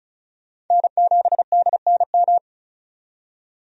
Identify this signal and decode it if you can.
Morse code, heard as a keyed tone, about 34 wpm, 700 Hz. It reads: N7DNM
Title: USA Call Signs - 5 Characters - 35wpm